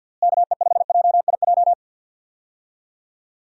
Morse code, heard as a keyed tone, about 53 wpm, 700 Hz. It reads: KE5JIJ
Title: USA Call Signs - 6 Characters - Repeated 3x - 50wpm